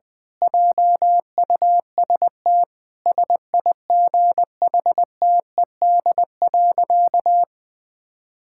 Morse code, heard as a keyed tone, about 20 wpm, 700 Hz. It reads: JUST SIGHTED.
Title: Pangram Challenge v2 - 20wpm